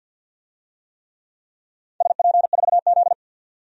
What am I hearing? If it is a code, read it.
Morse code, heard as a keyed tone, about 48 wpm, 700 Hz. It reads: SP4Z
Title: DXCC-100 Call Signs - 4 Characters - 50wpm